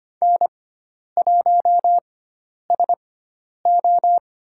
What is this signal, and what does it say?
Morse code, heard as a keyed tone, about 25 wpm, spaced as for 10 wpm, 700 Hz. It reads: N1SO
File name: Lesson 16g - Letter c - Call Signs 25wpm